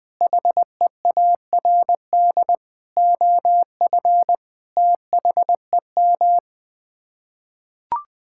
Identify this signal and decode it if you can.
Morse code, heard as a keyed tone, about 21 wpm, 700 Hz. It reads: HEARD OF THEM
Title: Sets of 9 Words - 20wpm